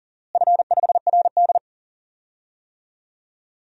Morse code, heard as a keyed tone, about 40 wpm, 700 Hz. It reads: F5RD
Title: DXCC-100 Call Signs - 4 Characters - Repeated 2x - 40wpm